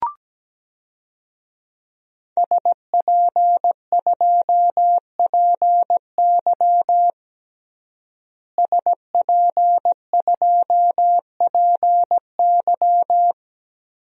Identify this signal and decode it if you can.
Morse code, heard as a keyed tone, about 17 wpm, 700 Hz. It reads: SP2PY SP2PY
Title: DXCC-100 Call Signs - 5 Characters - Repeated 2x - 17wpm